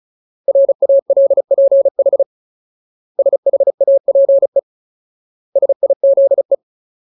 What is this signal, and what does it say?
Morse code, heard as a keyed tone, about 35 wpm, 550 Hz. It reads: RALPH SHAPE SIZE
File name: Ultimate Word List / 35WPM Ultimate